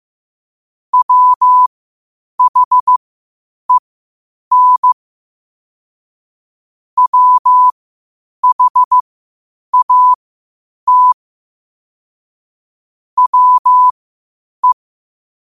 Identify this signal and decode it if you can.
Morse code, heard as a keyed tone, about 15 wpm, spaced as for 8 wpm, 1000 Hz. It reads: WHEN WHAT WE
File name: qotdfiles / 8wpm